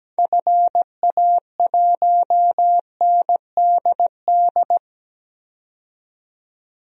Morse code, heard as a keyed tone, about 17 wpm, 700 Hz. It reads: FA1NDD
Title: DXCC-100 Call Signs - 6 Characters - 17wpm